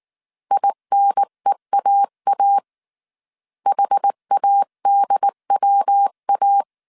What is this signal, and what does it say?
Morse code, heard as a keyed tone, about 19 wpm, 800 Hz. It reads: INEAA HADWA